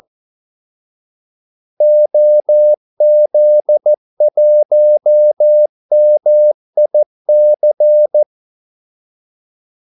Morse code, heard as a keyed tone, about 14 wpm, 600 Hz. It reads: OZ1MIC
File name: Callsigns-14wpm